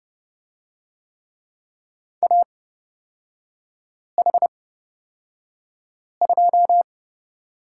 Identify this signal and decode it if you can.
Morse code, heard as a keyed tone, about 31 wpm, spaced as for 5 wpm, 690 Hz. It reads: AH2
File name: course / 30:10wpm / 236 Lesson 38e - Letter q - Three Characters 30wpm